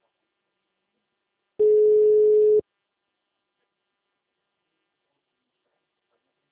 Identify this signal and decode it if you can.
Ringback tone (Europe, 425 Hz).